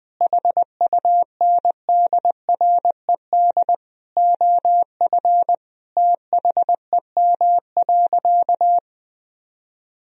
Morse code, heard as a keyed tone, about 20 wpm, 700 Hz. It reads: HUNDRED OF THEM.
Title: Sentences from Top 1000 Words - Review Part 2 - 20wpm